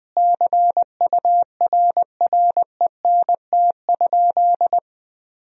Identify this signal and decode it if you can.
Morse code, heard as a keyed tone, about 20 wpm, 700 Hz. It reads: CURRENT?
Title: Sentences from Top 1000 Words - Review Part 4 - 20wpm